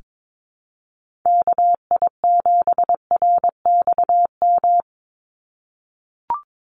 Morse code, heard as a keyed tone, about 22 wpm, 700 Hz. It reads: KI7RXM
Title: USA Call Signs 22wpm